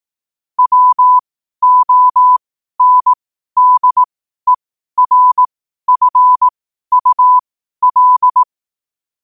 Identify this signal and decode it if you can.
Morse code, heard as a keyed tone, about 18 wpm, spaced as for 12 wpm, 1000 Hz. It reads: WONDERFUL
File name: qotdfiles / 12wpm